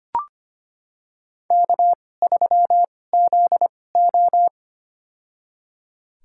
Morse code, heard as a keyed tone, about 25 wpm, spaced as for 18 wpm, 700 Hz. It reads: K3ZO
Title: USA Call Signs - 4 Characters - 2x Character Spacing - 25wpm